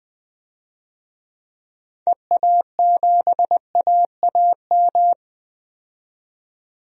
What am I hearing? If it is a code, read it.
Morse code, heard as a keyed tone, about 20 wpm, 700 Hz. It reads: EA7AAM
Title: DXCC-100 Call Signs - 6 Characters - 20wpm